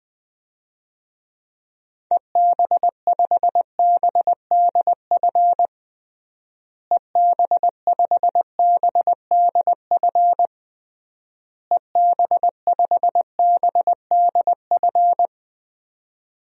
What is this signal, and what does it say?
Morse code, heard as a keyed tone, about 21 wpm, 700 Hz. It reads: EB5BDF EB5BDF EB5BDF
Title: DXCC-100 Call Signs - 6 Characters - Repeated 3x - 20wpm